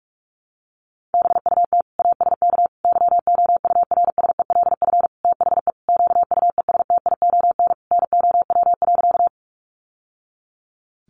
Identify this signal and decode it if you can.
Morse code, heard as a keyed tone, about 44 wpm, 700 Hz. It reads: BUT ASK YOURSELF THE QUESTION NOW.
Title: Sentences from Top 600 Words 45wpm